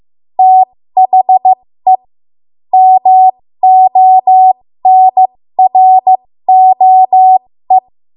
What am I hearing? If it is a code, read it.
Morse code, heard as a keyed tone, about 15 wpm, spaced as for 13 wpm, 750 Hz. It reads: THE MONROE
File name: Morse / 13WPM